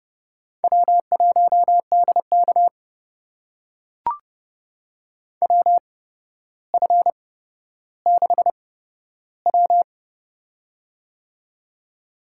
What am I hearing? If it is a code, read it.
Morse code, heard as a keyed tone, about 30 wpm, spaced as for 13 wpm, 700 Hz. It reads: W1DK WF6W
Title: USA Call Signs - 4 Characters - 8x Character Spacing - 30wpm